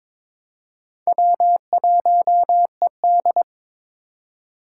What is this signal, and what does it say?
Morse code, heard as a keyed tone, about 22 wpm, 700 Hz. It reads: W1ED
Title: USA Call Signs - 4 Characters - 4x Character Spacing - 22wpm